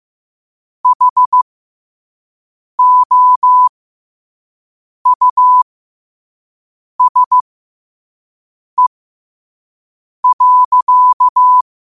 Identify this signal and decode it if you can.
Morse code, heard as a keyed tone, about 15 wpm, spaced as for 5 wpm, 1000 Hz. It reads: HOUSE.
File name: qotdfiles / 5wpm